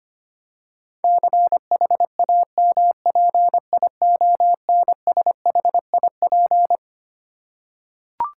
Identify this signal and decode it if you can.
Morse code, heard as a keyed tone, about 25 wpm, 700 Hz. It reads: CHAMPIONSHIP
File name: Twelve Letter Words - 25wpm